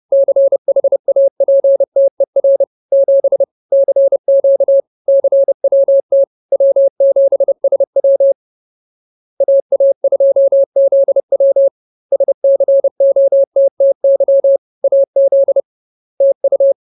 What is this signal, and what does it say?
Morse code, heard as a keyed tone, about 30 wpm, 550 Hz. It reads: CHAPTER 7 CQ CWT W7SW AA2ZW SCOTTY AZ TU